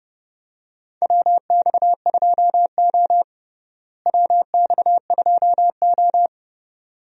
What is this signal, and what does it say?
Morse code, heard as a keyed tone, about 30 wpm, 700 Hz. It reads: WX2O WX2O
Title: USA Call Signs - 4 Characters - Repeated 2x - 30wpm